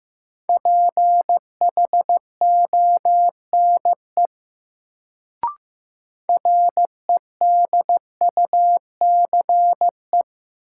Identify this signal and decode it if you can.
Morse code, heard as a keyed tone, about 15 wpm, 700 Hz. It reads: PHONE REDUCE
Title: Sentences from Top 1100 Words - New Word Drill Alt - 15wpm